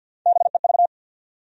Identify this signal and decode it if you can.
Morse code, heard as a keyed tone, about 47 wpm, 700 Hz. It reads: BEV
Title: USA Common Names - Max 5 Letters - 50wpm